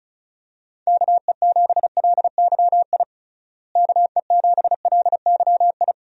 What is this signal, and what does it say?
Morse code, heard as a keyed tone, about 36 wpm, 700 Hz. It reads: KE7LYI KE7LYI
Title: USA Call Signs - 6 Characters - Repeated 2x - 35wpm